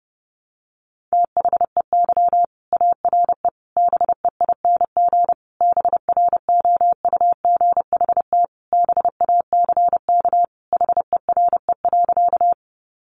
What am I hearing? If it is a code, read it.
Morse code, heard as a keyed tone, about 30 wpm, 700 Hz. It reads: THEY ARE BEING BROUGHT BACK HERE.
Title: Sentences from Top 600 Words 30wpm